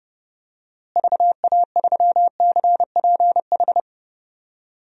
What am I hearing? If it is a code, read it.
Morse code, heard as a keyed tone, about 31 wpm, 700 Hz. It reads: VA3CPH
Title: DXCC-100 Call Signs - 6 Characters - Repeated 3x - 30wpm